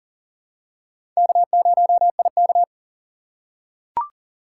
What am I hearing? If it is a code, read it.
Morse code, heard as a keyed tone, about 40 wpm, 700 Hz. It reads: K0IK
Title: USA Call Signs - 4 Characters - 2x Character Spacing - 40wpm